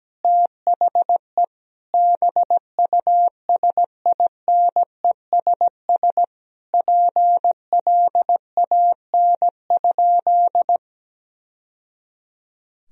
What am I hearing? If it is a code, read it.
Morse code, heard as a keyed tone, about 17 wpm, 700 Hz. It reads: THE BUSINESS PLAN?
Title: Sentences from Top 1000 Words - Review Part 3 - 17wpm